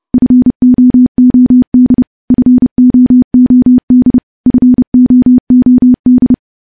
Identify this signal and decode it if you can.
Morse code, heard as a keyed tone, about 30 wpm, 250 Hz. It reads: FOOD FOOD FOOD
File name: morse / morse-training-30WPM